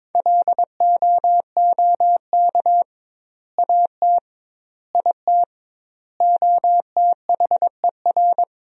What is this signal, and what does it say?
Morse code, heard as a keyed tone, about 22 wpm, 700 Hz. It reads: LOOK AT IT OTHER
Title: 2x Word-Spacing - Sentences from Top 200 Words - 22wpm